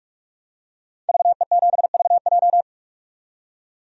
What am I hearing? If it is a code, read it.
Morse code, heard as a keyed tone, about 46 wpm, 700 Hz. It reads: VE7VJ
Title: DXCC-100 Call Signs - 5 Characters - 45wpm